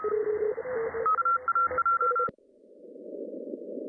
Morse code, heard as a keyed tone, about 32 wpm, 1370 Hz, with noise. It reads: AAJ
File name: CW / cqww-cw-k5tr